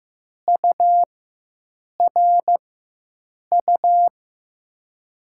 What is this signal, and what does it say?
Morse code, heard as a keyed tone, about 15 wpm, spaced as for 7 wpm, 700 Hz. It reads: URU
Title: USA Call Signs - 6 Characters - 4x Character Spacing - 15wpm